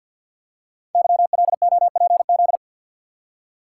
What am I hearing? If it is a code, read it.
Morse code, heard as a keyed tone, about 52 wpm, 700 Hz. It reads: YLOPZ
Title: Code Groups - 5 Letters - 50wpm